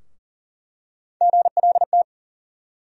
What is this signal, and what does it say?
Morse code, heard as a keyed tone, about 40 wpm, 700 Hz. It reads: GLT
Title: USA Call Sign Postfixes - 3 Letters 40wpm